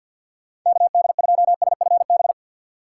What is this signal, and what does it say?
Morse code, heard as a keyed tone, about 50 wpm, 700 Hz. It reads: KD2SFB
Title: USA Call Signs - 6 Characters - 4x Character Spacing - 50wpm